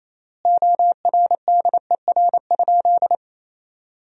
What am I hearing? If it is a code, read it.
Morse code, heard as a keyed tone, about 27 wpm, 700 Hz. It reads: ORDER?
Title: 3x Word-Spacing - Sentences from Top 400 Words - 28wpm